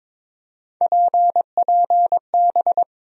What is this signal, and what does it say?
Morse code, heard as a keyed tone, about 22 wpm, 700 Hz. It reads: PPB
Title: USA Call Sign Postfixes - 3 Letters 22wpm